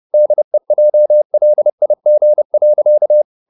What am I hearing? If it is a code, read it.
Morse code, heard as a keyed tone, about 30 wpm, 600 Hz. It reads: DEJLIG.
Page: Ferie 30 wpm